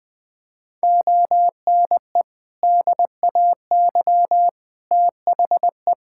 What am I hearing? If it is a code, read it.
Morse code, heard as a keyed tone, about 20 wpm, 700 Hz. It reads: ONE DAY THE
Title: Sentences from Top 1600 Words 20wpm